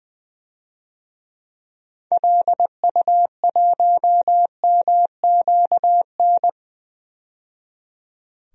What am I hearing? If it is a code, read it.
Morse code, heard as a keyed tone, about 20 wpm, 700 Hz. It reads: LU1MQN
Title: DXCC-100 Call Signs - 6 Characters - 20wpm